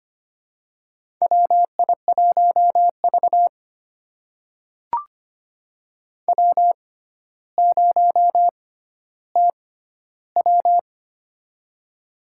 Morse code, heard as a keyed tone, about 25 wpm, spaced as for 13 wpm, 700 Hz. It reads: WI1V W0TW
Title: USA Call Signs - 4 Characters - 6x Character Spacing - 25wpm